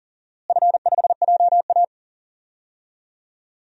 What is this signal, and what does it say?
Morse code, heard as a keyed tone, about 37 wpm, 700 Hz. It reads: F5JA
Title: DXCC-100 Call Signs - 4 Characters - Repeated 3x - 40wpm